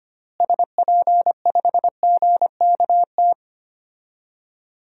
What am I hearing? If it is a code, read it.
Morse code, heard as a keyed tone, about 25 wpm, 700 Hz. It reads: SP5GKT
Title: DXCC-100 Call Signs - 6 Characters - Repeated 3x - 25wpm